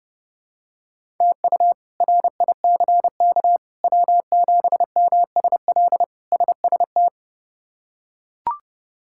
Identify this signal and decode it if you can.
Morse code, heard as a keyed tone, about 30 wpm, 700 Hz. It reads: TU RICK W7MSL SST
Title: K1USN SST QSO Practice - 30wpm